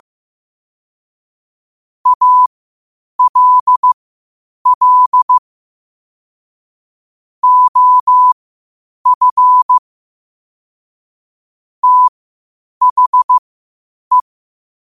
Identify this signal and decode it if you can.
Morse code, heard as a keyed tone, about 15 wpm, spaced as for 8 wpm, 1000 Hz. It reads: ALL OF THE